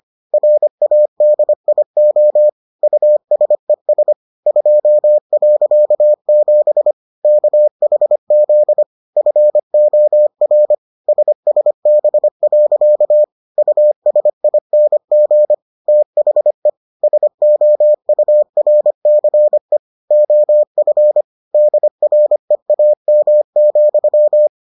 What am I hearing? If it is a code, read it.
Morse code, heard as a keyed tone, about 25 wpm, 600 Hz. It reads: RADIO USES 2.7 KHZ FOR SSB. USING THE SOURCE OF DREAM,